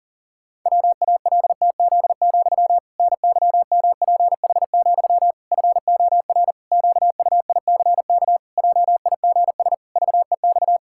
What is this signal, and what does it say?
Morse code, heard as a keyed tone, about 37 wpm, 700 Hz. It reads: WALTZ, NYMPH, FOR QUICK JIGS VEX
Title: Pangram Challenge v2 - 40wpm